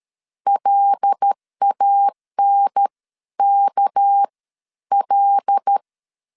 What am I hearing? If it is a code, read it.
Morse code, heard as a keyed tone, about 13 wpm, 800 Hz. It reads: LANK L
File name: CW / DET / 1 9 1500C S B 12wpm 3819